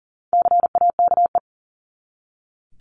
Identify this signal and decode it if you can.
Morse code, heard as a keyed tone, about 40 wpm, 700 Hz. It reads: CAKE
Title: Top 1501-1750 Words 40wpm